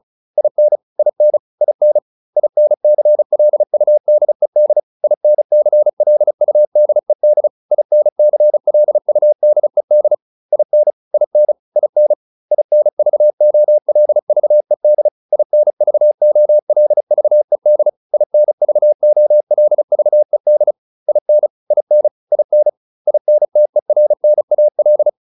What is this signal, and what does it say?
Morse code, heard as a keyed tone, about 34 wpm, 600 Hz. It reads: IN IN IN INCLUDED INCLUDED INCLUDED IN IN IN INVOLVED INVOLVED INVOLVED IN IN IN INTERNAL